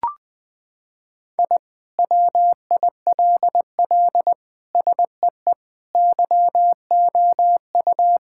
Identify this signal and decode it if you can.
Morse code, heard as a keyed tone, about 20 wpm, 700 Hz. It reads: I WILL SEE YOU
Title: Sentences with 2-Word Phrases - 20wpm